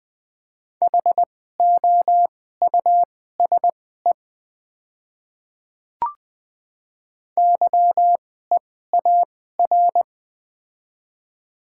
Morse code, heard as a keyed tone, about 20 wpm, spaced as for 14 wpm, 700 Hz. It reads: HOUSE YEAR
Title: 2x Character Spacing - Top 500 Words - 20wpm